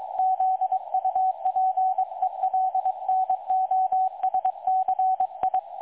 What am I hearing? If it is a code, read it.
Morse code, heard as a keyed tone, about 22 wpm, 750 Hz. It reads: ZUPELNOSCI
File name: dlazaawansowanych / nagraniazpasma / websdr recording 3565.0kHz